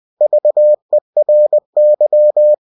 Morse code, heard as a keyed tone, about 20 wpm, 600 Hz. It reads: VERY
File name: cw-books / winnie-the-pooh / winnie-20wpm-page-0081